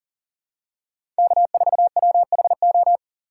Morse code, heard as a keyed tone, about 37 wpm, 700 Hz. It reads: K4WHO
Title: USA Call Signs - 5 Characters - 6x Character Spacing - 40wpm